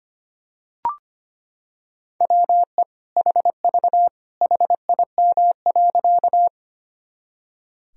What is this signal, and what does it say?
Morse code, heard as a keyed tone, about 25 wpm, 700 Hz. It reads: WE HV HIM.
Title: Sentences from Top 100 Words and Abbreviations 25wpm